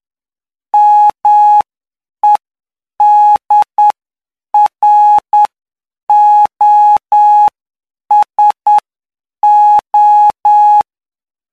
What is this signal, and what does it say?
Morse code, heard as a keyed tone, about 9 wpm, spaced as for 7 wpm, 840 Hz. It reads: MEDROSO
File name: cw / py2lcd / 04-PY2LCD-Li��o 13